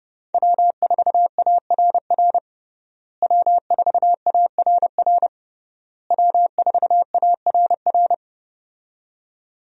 Morse code, heard as a keyed tone, about 29 wpm, 700 Hz. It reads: W4ARR W4ARR W4ARR
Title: USA Call Signs - 5 Characters - Repeated 3x - 30wpm